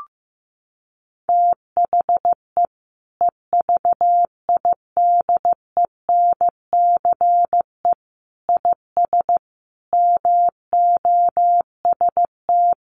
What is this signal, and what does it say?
Morse code, heard as a keyed tone, about 15 wpm, 700 Hz. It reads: THE EVIDENCE IS MOST
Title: Sentences from Top 1300 Words 15wpm